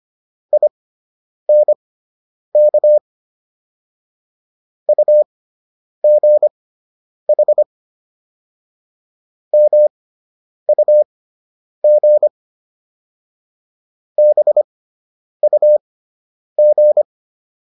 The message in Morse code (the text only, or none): INK UGH MUG BUG